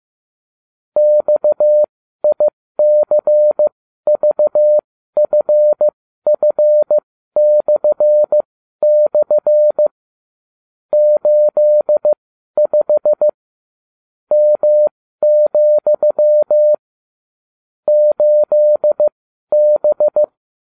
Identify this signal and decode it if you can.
Morse code, heard as a keyed tone, about 15 wpm, spaced as for 12 wpm, 600 Hz. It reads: XICVFF// 85 M, 8B